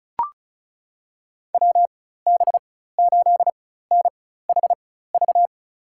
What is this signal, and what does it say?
Morse code, heard as a keyed tone, about 36 wpm, spaced as for 16 wpm, 700 Hz. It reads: WB8NHV
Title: USA Call Signs - 6 Characters - 4x Character Spacing - 35wpm